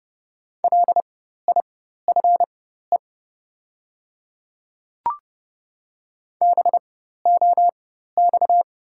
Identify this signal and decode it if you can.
Morse code, heard as a keyed tone, about 31 wpm, spaced as for 14 wpm, 690 Hz. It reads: LIFE BOX
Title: 4x Character Spacing - Top 500 Words - 30wpm